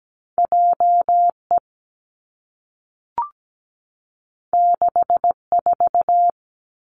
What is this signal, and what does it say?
Morse code, heard as a keyed tone, about 17 wpm, 700 Hz. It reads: JE 64
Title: Two Challenging Letter-Number Combinations 17wpm